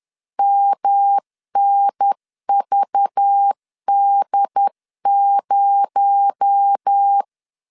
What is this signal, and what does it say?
Morse code, heard as a keyed tone, about 10 wpm, 800 Hz. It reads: MNVD0
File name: CW / DET / 1 6 1500C S C 10wpm 4252